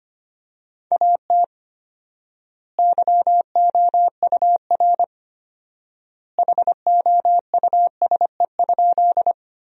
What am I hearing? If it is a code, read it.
Morse code, heard as a keyed tone, about 25 wpm, 700 Hz. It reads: AT YOUR HOUSE?
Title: 4x Word-Spacing - Sentences from Top 200 Words - 25wpm